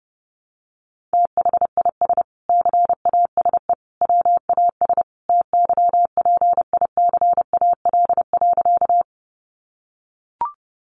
Morse code, heard as a keyed tone, about 30 wpm, 700 Hz. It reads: THIS CASE WAS TYPICAL.
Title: Sentences from Top 1100 Words 30wpm